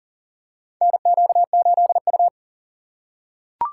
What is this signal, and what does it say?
Morse code, heard as a keyed tone, about 40 wpm, 700 Hz. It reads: NQ8U
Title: USA Call Signs - 4 Characters - 2x Character Spacing - 40wpm